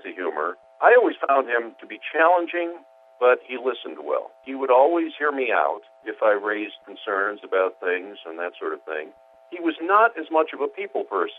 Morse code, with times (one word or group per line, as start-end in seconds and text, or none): none